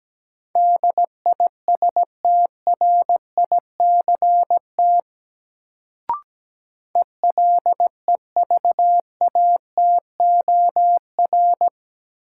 Morse code, heard as a keyed tone, about 17 wpm, 700 Hz. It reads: DISTRICT ELEVATOR
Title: Sentences from Top 1600 Words - New Word Drill - 17wpm